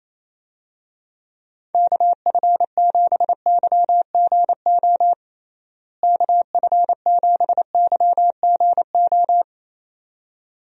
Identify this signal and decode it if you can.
Morse code, heard as a keyed tone, about 28 wpm, 700 Hz. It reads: KF7YGO KF7YGO
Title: USA Call Signs - 6 Characters - Repeated 2x - 28wpm